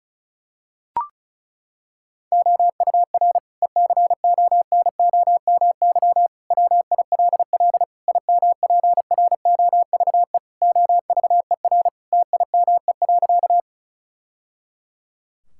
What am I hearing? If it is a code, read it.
Morse code, heard as a keyed tone, about 36 wpm, 690 Hz. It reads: OUR ECONOMY WILL IMPROVE OVER TIME.
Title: Sentences from Top 1000 Words - Review Part 5 - 35wpm